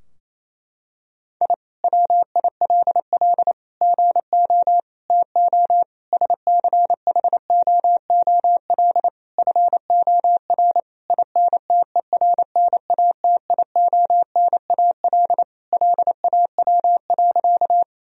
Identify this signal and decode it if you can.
Morse code, heard as a keyed tone, about 28 wpm, 700 Hz. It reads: I WILL GO TO SCHOOL FOR INTERNATIONAL LAW.
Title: Sentences from Top 1000 Words - Review Part 4 - 28wpm